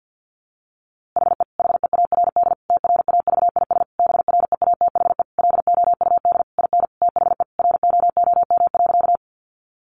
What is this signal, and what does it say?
Morse code, heard as a keyed tone, about 50 wpm, 700 Hz. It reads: HE HEARD TRAVIS BREATHE LOUD IN THE ROOM.
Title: A Sound Of Thunder 50wpm